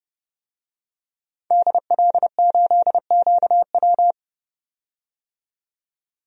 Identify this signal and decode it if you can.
Morse code, heard as a keyed tone, about 30 wpm, 700 Hz. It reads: DL8QW
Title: DXCC-100 Call Signs - 5 Characters - 30wpm